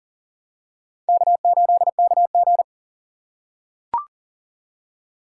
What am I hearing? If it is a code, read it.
Morse code, heard as a keyed tone, about 40 wpm, 700 Hz. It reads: K8KG
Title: USA Call Signs - 4 Characters - 4x Character Spacing - 40wpm